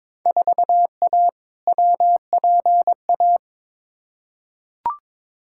Morse code, heard as a keyed tone, about 21 wpm, 700 Hz. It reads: 4A WPA
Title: Field Day Exchanges Encore - 22wpm